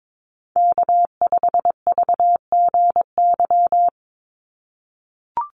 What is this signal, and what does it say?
Morse code, heard as a keyed tone, about 22 wpm, 700 Hz. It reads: K5VGY
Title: USA Call Signs 22wpm